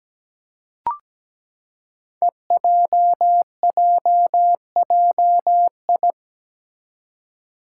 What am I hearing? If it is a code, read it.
Morse code, heard as a keyed tone, about 17 wpm, 700 Hz. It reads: EJJJI
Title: Five Challenging Letter-Number Combinations 17wpm